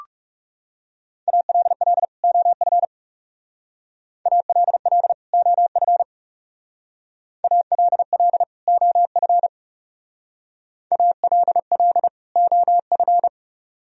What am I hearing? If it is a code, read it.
Morse code, heard as a keyed tone, about 36 wpm, 700 Hz. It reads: ALL OF ALL OF ALL OF ALL OF
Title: Speed-Racing - Sets of 2 Words 30wpm